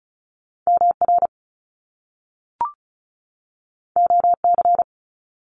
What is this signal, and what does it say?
Morse code, heard as a keyed tone, about 36 wpm, 700 Hz. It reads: MR OC
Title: Top Two Letter Combinations in Words 35wpm